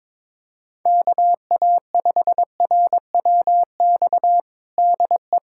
Morse code, heard as a keyed tone, about 22 wpm, 700 Hz. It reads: KA5RWX DE
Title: USA Callsign DE Callsign - 6 Characters - 22wpm